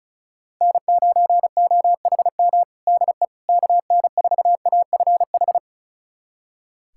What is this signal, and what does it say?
Morse code, heard as a keyed tone, about 34 wpm, 700 Hz. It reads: N9OHM DE KN4AFH
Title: USA Callsign DE Callsign 35wpm